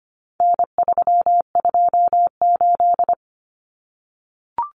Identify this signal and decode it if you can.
Morse code, heard as a keyed tone, about 25 wpm, 700 Hz. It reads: N328
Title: Common Cut Numbers - 4-Digit 25wpm